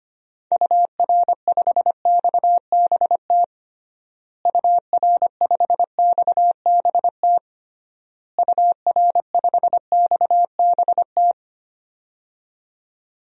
Morse code, heard as a keyed tone, about 25 wpm, 700 Hz. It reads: UR5XBT UR5XBT UR5XBT
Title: DXCC-100 Call Signs - 6 Characters - Repeated 3x - 25wpm